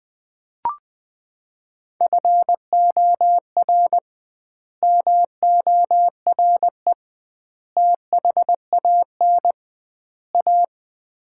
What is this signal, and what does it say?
Morse code, heard as a keyed tone, about 21 wpm, 700 Hz. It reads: FOR MORE THAN A
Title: 2x Word-Spacing - Sets of 4 Words - 20wpm